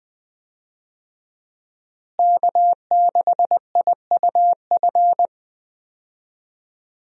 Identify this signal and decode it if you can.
Morse code, heard as a keyed tone, about 20 wpm, 700 Hz. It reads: K6IUF
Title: USA Call Signs - 5 Characters - Repeated 2x - 20wpm